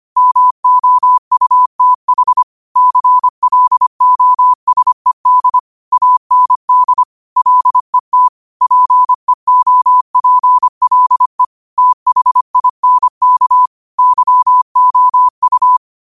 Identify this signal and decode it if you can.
Morse code, heard as a keyed tone, about 25 wpm, 1000 Hz. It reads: MOUTH CLOSED AND LET PEOPLE THINK YOU
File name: qotdfiles / 25wpm